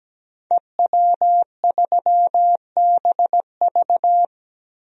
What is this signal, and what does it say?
Morse code, heard as a keyed tone, about 17 wpm, 700 Hz. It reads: EW3BV
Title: DXCC-100 Call Signs - 5 Characters - Repeated 2x - 17wpm